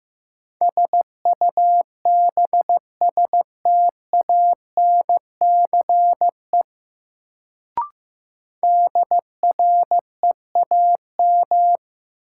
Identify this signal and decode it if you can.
Morse code, heard as a keyed tone, about 15 wpm, 700 Hz. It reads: SUBSTANCE DREAM
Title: Sentences from Top 1500 Words - New Word Drill Alt - 15wpm